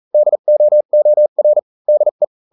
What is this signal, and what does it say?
Morse code, heard as a keyed tone, about 40 wpm, spaced as for 35 wpm, 600 Hz. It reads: DOOR DE